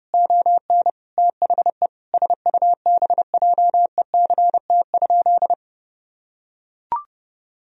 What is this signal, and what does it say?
Morse code, heard as a keyed tone, about 30 wpm, 700 Hz. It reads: ON THE SUBJECT?
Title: Sentences from Top 1000 Words - Review Part 1 - 30wpm